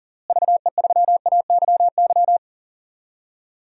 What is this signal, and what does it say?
Morse code, heard as a keyed tone, about 37 wpm, 700 Hz. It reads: VE3AYY
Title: DXCC-100 Call Signs - 6 Characters - Repeated 3x - 40wpm